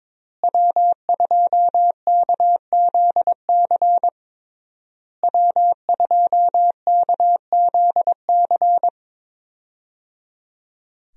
Morse code, heard as a keyed tone, about 22 wpm, 700 Hz. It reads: W2KZC W2KZC
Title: USA Call Signs - 5 Characters - Repeated 3x - 22wpm